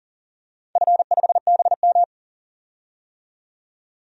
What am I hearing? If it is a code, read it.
Morse code, heard as a keyed tone, about 40 wpm, 700 Hz. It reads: F5BM
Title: DXCC-100 Call Signs - 4 Characters - 40wpm